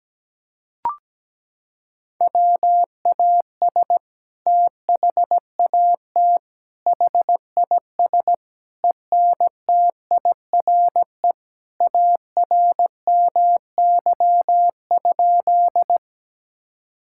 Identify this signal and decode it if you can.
Morse code, heard as a keyed tone, about 17 wpm, 700 Hz. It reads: WAS THAT HIS ENTIRE ARMY?
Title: Sentences from Top 1000 Words - Review Part 5 - 17wpm